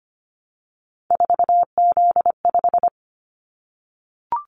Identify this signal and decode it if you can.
Morse code, heard as a keyed tone, about 25 wpm, 700 Hz. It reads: 4Z5
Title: Callsign Prefixes for Top 50 DXCC - 25wpm